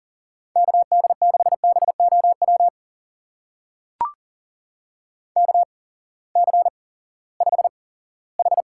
Morse code, heard as a keyed tone, about 41 wpm, spaced as for 20 wpm, 700 Hz. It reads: KD6BOW KC5H
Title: USA Call Signs - 6 Characters - 8x Character Spacing - 40wpm